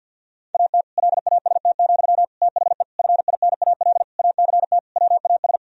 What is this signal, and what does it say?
Morse code, heard as a keyed tone, about 52 wpm, 700 Hz. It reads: AT LAST, THE FINAL ACT WAS
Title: Sentences from Top 1000 Words - Review Part 2 - 50wpm